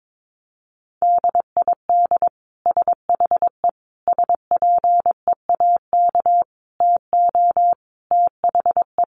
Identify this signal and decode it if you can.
Morse code, heard as a keyed tone, about 22 wpm, 700 Hz. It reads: DID SHE SPEAK TO THE
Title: Sentences from Top 1200 Words 22wpm